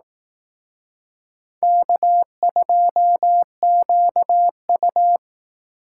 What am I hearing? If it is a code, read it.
Morse code, heard as a keyed tone, about 18 wpm, 700 Hz. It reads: K2QU
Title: Speed Racing - USA 1x2 Call Signs 15wpm